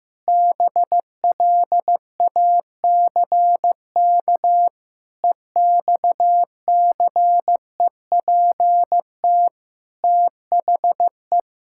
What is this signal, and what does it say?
Morse code, heard as a keyed tone, about 15 wpm, 700 Hz. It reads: BLACK EXCEPT THE
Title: Sentences from Top 1600 Words 15wpm